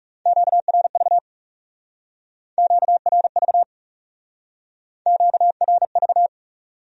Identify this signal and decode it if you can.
Morse code, heard as a keyed tone, about 36 wpm, 700 Hz. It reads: QRV QRV QRV
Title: Speed-Racing Q-Codes - 30wpm